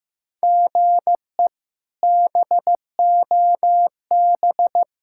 Morse code, heard as a keyed tone, about 15 wpm, 700 Hz. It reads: GE BOB
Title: K1USN SST QSO Practice - 15wpm